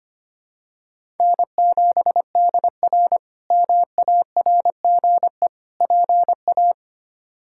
Morse code, heard as a keyed tone, about 25 wpm, 700 Hz. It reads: N7DR MARGE PA
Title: K1USN SST QSO Practice - 25wpm